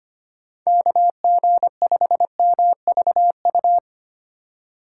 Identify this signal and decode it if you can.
Morse code, heard as a keyed tone, about 25 wpm, 700 Hz. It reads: KG5MVU
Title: USA Call Signs - 6 Characters - 3x Character Spacing - 25wpm